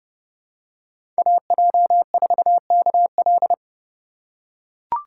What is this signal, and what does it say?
Morse code, heard as a keyed tone, about 31 wpm, 700 Hz. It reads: AJ4KL
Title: USA Call Signs - 5 Characters - 3x Character Spacing - 30wpm